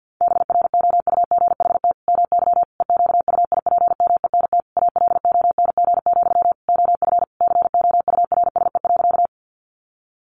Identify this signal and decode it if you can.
Morse code, heard as a keyed tone, about 50 wpm, 700 Hz. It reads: BROUGHT MY EQUIPMENT ALONG, OF COURSE.
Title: Sentences from Top 700 Words 50wpm